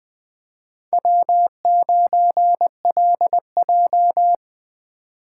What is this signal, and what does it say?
Morse code, heard as a keyed tone, about 20 wpm, 700 Hz. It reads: W9LJ
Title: Speed Racing - USA 1x2 Call Signs 17wpm